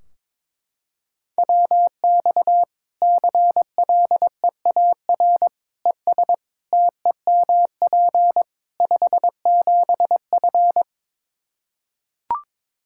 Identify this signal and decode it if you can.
Morse code, heard as a keyed tone, about 22 wpm, 700 Hz. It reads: WX CLEAR ES TEMP 57F
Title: Practice QSO - Exchange 2 - Info - 22wpm